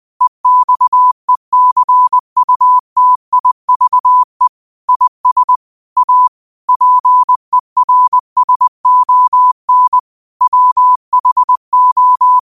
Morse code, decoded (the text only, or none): EXECUTIVE IS A PERSON WHO